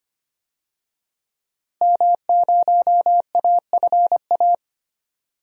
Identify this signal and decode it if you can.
Morse code, heard as a keyed tone, about 25 wpm, 700 Hz. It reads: M0AFA
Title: DXCC-100 Call Signs - 5 Characters - Repeated 2x - 25wpm